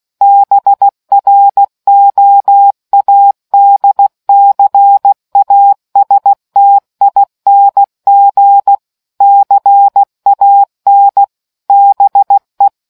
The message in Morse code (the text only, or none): BROADCASTING CAN BE